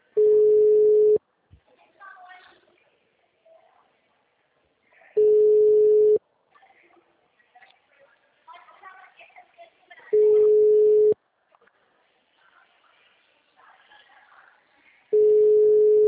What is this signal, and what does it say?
Ringback tone (Europe, 425 Hz), with speech.
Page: Desbravador 1